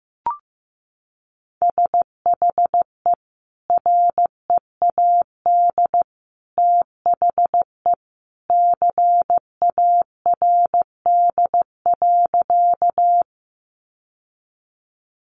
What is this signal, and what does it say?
Morse code, heard as a keyed tone, about 15 wpm, 700 Hz. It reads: SHE READ THE CARD.
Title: Sentences from Top 600 Words 15wpm